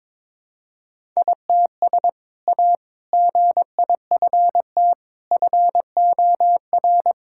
Morse code, heard as a keyed tone, about 22 wpm, 700 Hz. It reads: ITS A GIFT FOR
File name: practice / 22wpm / Sentences with 2-Word Phrases - 22wpm